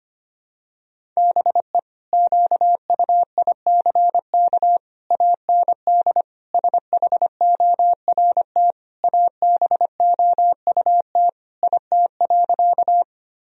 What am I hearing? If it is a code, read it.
Morse code, heard as a keyed tone, about 25 wpm, 700 Hz. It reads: BE QUICK AND SHORT ABOUT IT.
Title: Sentences from Top 1000 Words - Review Part 2 - 25wpm